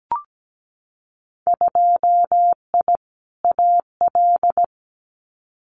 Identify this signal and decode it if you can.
Morse code, heard as a keyed tone, about 17 wpm, 700 Hz. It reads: 2I AL
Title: Winter Field Day Exchanges 17wpm